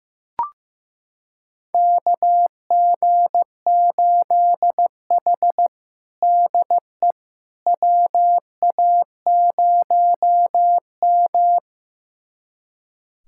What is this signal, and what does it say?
Morse code, heard as a keyed tone, about 15 wpm, 700 Hz. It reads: KG8H DE WA0M
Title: USA Callsign DE Callsign - 4 Characters - 15wpm